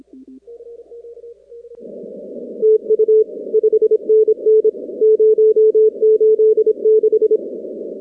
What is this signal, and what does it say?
Morse code, heard as a keyed tone, about 27 wpm, 440 Hz. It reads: TU 5NN 086
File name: contests / 2019-cw-wpx-kt5j-k5pi- -2330 0-19